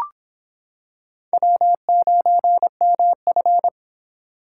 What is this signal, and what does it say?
Morse code, heard as a keyed tone, about 25 wpm, 700 Hz. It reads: W9MF